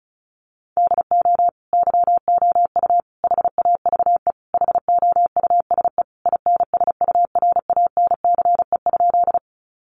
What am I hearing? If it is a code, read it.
Morse code, heard as a keyed tone, about 34 wpm, 700 Hz. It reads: DO YOU HAVE HOUSE INSURANCE?
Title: Sentences from Top 1100 Words 35wpm